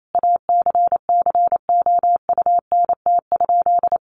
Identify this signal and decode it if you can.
Morse code, heard as a keyed tone, about 28 wpm, 700 Hz. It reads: ACCOUNT?
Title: Sentences from Top 900 Words 28wpm